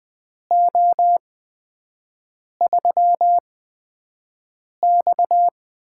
Morse code, heard as a keyed tone, about 20 wpm, spaced as for 5 wpm, 700 Hz. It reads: O3X
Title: USA Call Signs - 8x Character Spacing - 20wpm